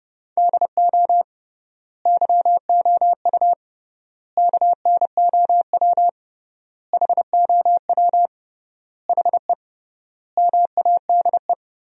Morse code, heard as a keyed tone, about 30 wpm, 700 Hz. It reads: DO YOU KNOW HOW HE MADE
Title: 3x Word-Spacing - Sentences from Top 300 Words - 30wpm